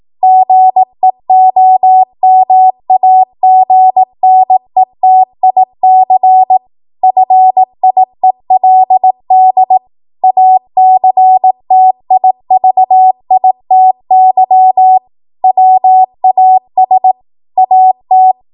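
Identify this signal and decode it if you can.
Morse code, heard as a keyed tone, about 18 wpm, 750 Hz. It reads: GEOMAGNETIC FIELD ACTIVITY WAS AT
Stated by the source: 18 WPM Morse code practice from a national amateur radio association's code archive.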